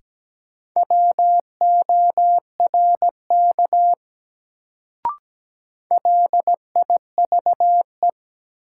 Morse code, heard as a keyed tone, about 17 wpm, 700 Hz. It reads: WORK LIVE
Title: Sentences from Top 200 Words - New Word Drill - 17wpm